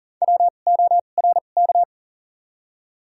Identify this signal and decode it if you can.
Morse code, heard as a keyed tone, about 40 wpm, spaced as for 29 wpm, 700 Hz. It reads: WORK
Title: 2x Character Spacing - Top 300 Words - 40wpm